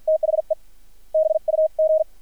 Morse code, heard as a keyed tone, about 50 wpm, spaced as for 44 wpm, 630 Hz. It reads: THE BUG
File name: archived / CRSnet